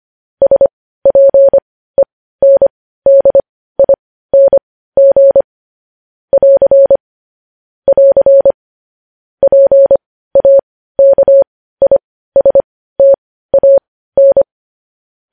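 Morse code, heard as a keyed tone, about 25 wpm, spaced as for 15 wpm, 550 Hz. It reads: SPENDING <AR> <AR> PAKISTAN
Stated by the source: Morse code news headlines at 15.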